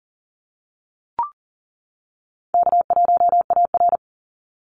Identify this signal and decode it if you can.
Morse code, heard as a keyed tone, about 40 wpm, 700 Hz. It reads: K1AR
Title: Easy Call Signs - 40wpm